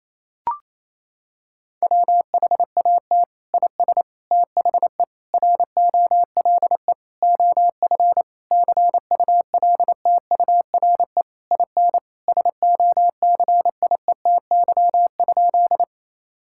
Morse code, heard as a keyed tone, about 28 wpm, 700 Hz. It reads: WHAT IS THE ROLE OF CULTURE IN SOCIETY?
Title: Sentences from Top 1000 Words - Review Part 6 - 28wpm